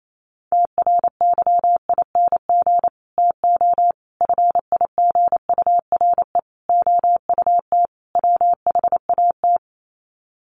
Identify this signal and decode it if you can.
Morse code, heard as a keyed tone, about 28 wpm, 700 Hz. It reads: TRYING TO FIGURE OUT WHAT
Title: Sets of 5 Words 28wpm